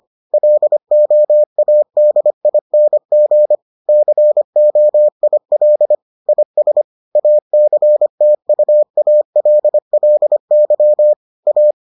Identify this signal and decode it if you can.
Morse code, heard as a keyed tone, about 25 wpm, 600 Hz. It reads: LOADING COIL IS ACTUALLY A